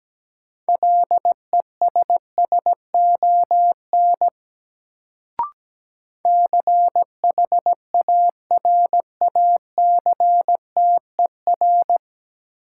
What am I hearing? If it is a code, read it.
Morse code, heard as a keyed tone, about 17 wpm, 700 Hz. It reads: LESSON CHARACTER
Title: Sentences from Top 1200 Words - New Word Drill - 17wpm